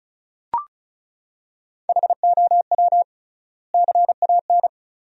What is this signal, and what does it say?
Morse code, heard as a keyed tone, about 36 wpm, 700 Hz. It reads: HOW CAN
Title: 3x Word-Spacing - Sentences from Top 200 Words - 35wpm